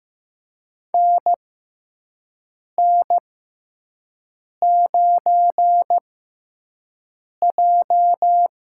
Morse code, heard as a keyed tone, about 15 wpm, spaced as for 5 wpm, 700 Hz. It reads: NN9J
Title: USA Call Signs - 6x Character Spacing - 15wpm